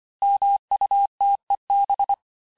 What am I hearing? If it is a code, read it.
Morse code, heard as a keyed tone, about 25 wpm, 800 Hz. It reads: MUTEB
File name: texte-chiffre-vitesse-23mpm